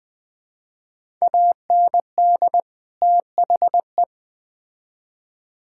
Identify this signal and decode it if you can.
Morse code, heard as a keyed tone, about 20 wpm, 700 Hz. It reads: AND THE
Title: Build A Phrase - 3-Word Phrases - 20wpm